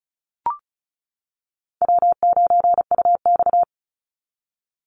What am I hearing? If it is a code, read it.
Morse code, heard as a keyed tone, about 34 wpm, 700 Hz. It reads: W9UX
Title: NA SOTA Chaser Callsigns 35wpm